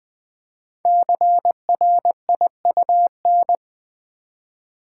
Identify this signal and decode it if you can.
Morse code, heard as a keyed tone, about 20 wpm, 700 Hz. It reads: CRIUN
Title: Code Groups - 5 Letters - 20wpm